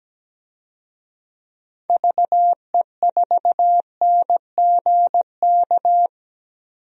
Morse code, heard as a keyed tone, about 17 wpm, 700 Hz. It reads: VE4NGK
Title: DXCC-100 Call Signs - 6 Characters - 17wpm